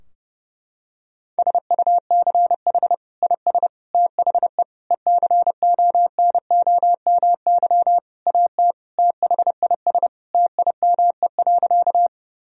Morse code, heard as a keyed tone, about 30 wpm, 700 Hz. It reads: SUCH IS THE ECONOMY AT THIS TIME.
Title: Sentences from Top 1000 Words - Review Part 3 - 30wpm